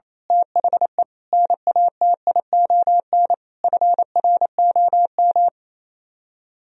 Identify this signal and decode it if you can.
Morse code, heard as a keyed tone, about 26 wpm, 700 Hz. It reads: THE NATION FROM
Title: Speed-Racing - Sets of 5 Words 28wpm